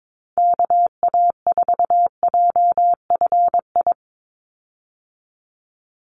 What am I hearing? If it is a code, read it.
Morse code, heard as a keyed tone, about 22 wpm, 700 Hz. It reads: KA4JFI
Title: USA Call Signs 22wpm